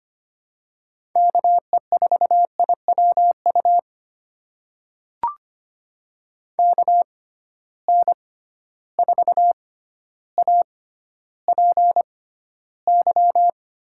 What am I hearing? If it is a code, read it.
Morse code, heard as a keyed tone, about 25 wpm, spaced as for 13 wpm, 700 Hz. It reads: KE4IWU KN4APY
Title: USA Call Signs - 6 Characters - 6x Character Spacing - 25wpm